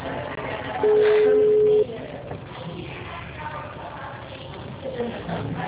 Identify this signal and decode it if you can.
Ringback tone (Europe, 425 Hz).